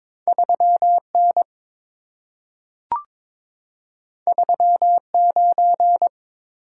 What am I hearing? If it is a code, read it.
Morse code, heard as a keyed tone, about 22 wpm, 690 Hz. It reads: 3N 39
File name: practice / 22wpm / Two Letter-Number Combinations - 22wpm